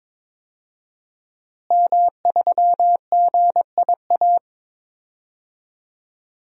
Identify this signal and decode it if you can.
Morse code, heard as a keyed tone, about 21 wpm, 700 Hz. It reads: M3GIA
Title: DXCC-100 Call Signs - 5 Characters - Repeated 2x - 22wpm